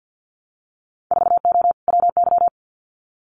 Morse code, heard as a keyed tone, about 50 wpm, 700 Hz. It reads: 4O WY
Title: Winter Field Day Exchanges Encore 50wpm